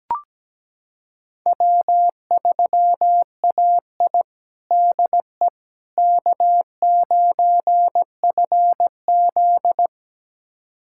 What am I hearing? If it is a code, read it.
Morse code, heard as a keyed tone, about 17 wpm, 700 Hz. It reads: W3AI DE K9FZ
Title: USA Callsign DE Callsign - 4 Characters - 17wpm